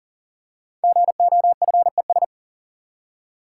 Morse code, heard as a keyed tone, about 38 wpm, 700 Hz. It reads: GOFES
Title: Code Groups - 5 Letters - 40wpm